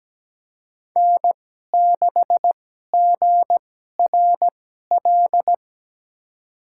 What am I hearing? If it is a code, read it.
Morse code, heard as a keyed tone, about 17 wpm, spaced as for 12 wpm, 700 Hz. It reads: N6GRL
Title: USA Call Signs - 5 Characters - 2x Character Spacing - 17wpm